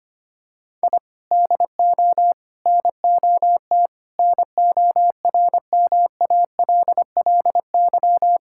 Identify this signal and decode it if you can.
Morse code, heard as a keyed tone, about 25 wpm, 700 Hz. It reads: I DO NOT NORMALLY
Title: Sentences from Top 1000 Words - Review Part 6 - 25wpm